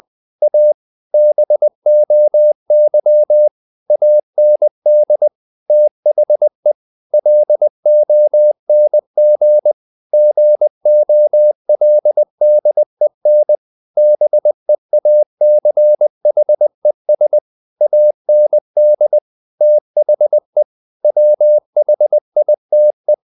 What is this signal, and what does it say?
Morse code, heard as a keyed tone, about 20 wpm, 600 Hz. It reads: A BOY AND THE LONG GOLDEN BEACHES AND THE WHITE
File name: the old man and the sea / the old man and the sea part 1 20wpm